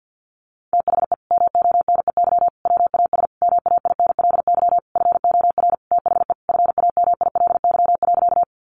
Morse code, heard as a keyed tone, about 50 wpm, 700 Hz. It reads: THE MONEY WAS MAINLY FOR THE FAMILY.
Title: Sentences from Top 1400 Words 50wpm